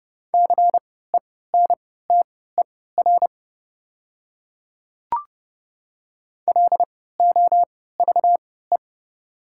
Morse code, heard as a keyed tone, about 30 wpm, spaced as for 17 wpm, 690 Hz. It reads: CENTER LOVE
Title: 3x Character Spacing - Top 500 Words - 30wpm